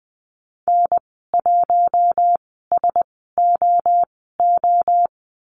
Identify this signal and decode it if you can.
Morse code, heard as a keyed tone, about 20 wpm, spaced as for 14 wpm, 700 Hz. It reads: N1SOO
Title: 042 Lesson 8g - Number 1 - Call Signs 20wpm